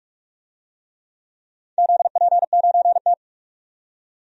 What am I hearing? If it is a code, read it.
Morse code, heard as a keyed tone, about 46 wpm, 700 Hz. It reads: ZP9T
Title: DXCC-100 Call Signs - 4 Characters - 45wpm